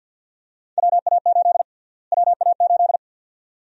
Morse code, heard as a keyed tone, about 50 wpm, 700 Hz. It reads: WA8 WA8
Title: USA Call Sign Prefixes - Repeated 2x - 50wpm